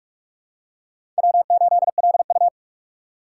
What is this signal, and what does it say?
Morse code, heard as a keyed tone, about 46 wpm, 700 Hz. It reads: W8LU
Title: USA Call Signs - 4 Characters - 4x Character Spacing - 45wpm